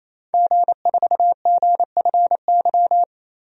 Morse code, heard as a keyed tone, about 27 wpm, 700 Hz. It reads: G4GFY
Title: DXCC-100 Call Signs - 5 Characters - 28wpm